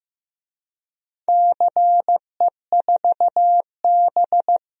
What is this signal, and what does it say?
Morse code, heard as a keyed tone, about 15 wpm, 700 Hz. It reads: CE4B
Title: DXCC-100 Call Signs - 4 Characters - Repeated 2x - 15wpm